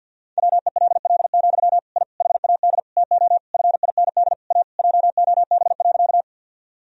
Morse code, heard as a keyed tone, about 52 wpm, 700 Hz. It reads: WELL, I HAD TO FIND A JOB.
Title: Sentences from Top 1000 Words - Review Part 3 - 50wpm